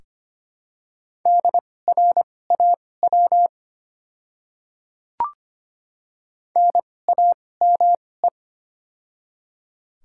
Morse code, heard as a keyed tone, about 25 wpm, spaced as for 18 wpm, 700 Hz. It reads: DRAW NAME
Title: 2x Character Spacing - Top 300 Words - 25wpm